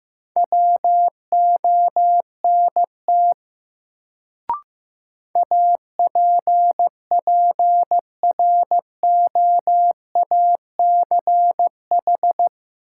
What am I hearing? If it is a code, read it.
Morse code, heard as a keyed tone, about 15 wpm, 700 Hz. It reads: WONT APPROACH
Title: Sentences from Top 1000 Words - Review Part 6 - New Word Drill - 15wpm